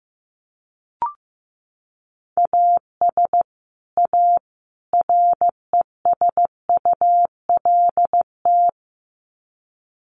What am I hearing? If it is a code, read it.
Morse code, heard as a keyed tone, about 15 wpm, 700 Hz. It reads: AS A RESULT
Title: Sets of 3 Words 15wpm